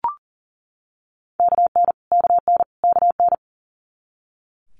Morse code, heard as a keyed tone, about 39 wpm, 700 Hz. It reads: KN KN KN
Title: Multi-letter Phonemes Repeated 3x 40wpm